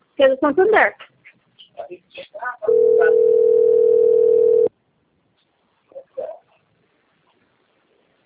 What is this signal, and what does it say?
Ringback tone (North America), 440 + 480 Hz, with speech.
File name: rg-1600-unknown- - - .38190